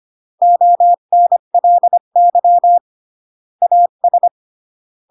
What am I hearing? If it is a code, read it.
Morse code, heard as a keyed tone, about 25 wpm, 700 Hz. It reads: ONLY AS
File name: CW / jodorowsky / 82 commandments 25-20